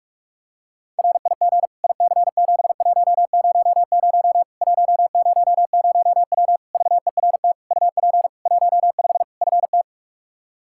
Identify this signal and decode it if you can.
Morse code, heard as a keyed tone, about 43 wpm, 700 Hz. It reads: RIG IC7100 100W VERT UP 15 FT
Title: K1USN SST QSO Practice 45wpm